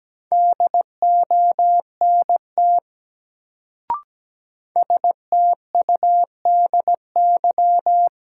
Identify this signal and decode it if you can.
Morse code, heard as a keyed tone, about 17 wpm, 700 Hz. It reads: DONT STUDY
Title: Sentences from Top 300 Words - New Word Drill - 17wpm